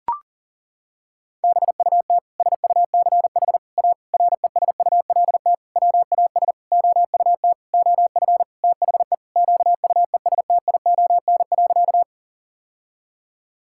Morse code, heard as a keyed tone, about 37 wpm, 700 Hz. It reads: BUT SUCH A RESULT WAS OUT OF THE QUESTION.
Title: Sentences from Top 1000 Words - Review Part 5 - 40wpm